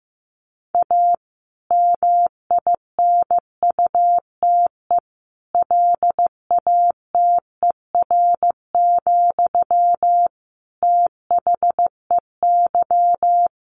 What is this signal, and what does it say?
Morse code, heard as a keyed tone, about 15 wpm, 700 Hz. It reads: A MINUTE LATER, THEY
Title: Sentences from Top 1500 Words 15wpm